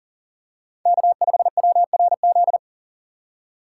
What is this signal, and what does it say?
Morse code, heard as a keyed tone, about 38 wpm, 700 Hz. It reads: K5WRZ
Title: USA Call Signs - 5 Characters - 4x Character Spacing - 40wpm